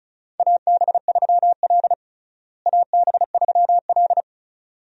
Morse code, heard as a keyed tone, about 34 wpm, 700 Hz. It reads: AB3L AB3L
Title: USA Call Signs - 4 Characters - Repeated 2x - 35wpm